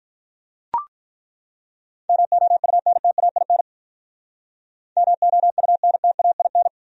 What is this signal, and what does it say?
Morse code, heard as a keyed tone, about 43 wpm, 700 Hz. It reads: MOUNTAIN MOUNTAIN
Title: Speed-Racing Top 500 Words 35wpm